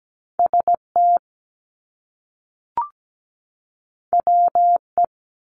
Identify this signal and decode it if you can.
Morse code, heard as a keyed tone, about 17 wpm, 700 Hz. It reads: ST WE
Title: Top Two Letter Combinations in Words 17wpm